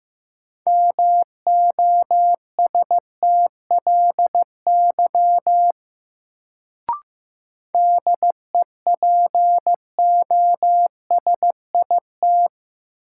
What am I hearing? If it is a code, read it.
Morse code, heard as a keyed tone, about 15 wpm, 700 Hz. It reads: MOSTLY DEPOSIT
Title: Sentences from Top 1300 Words - New Word Drill - 15wpm